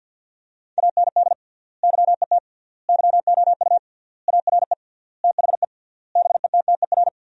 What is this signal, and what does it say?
Morse code, heard as a keyed tone, about 53 wpm, 700 Hz. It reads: AND YET YOU ARE THE BETTER
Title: 3x Word-Spacing - Sentences from Top 500 Words - 50wpm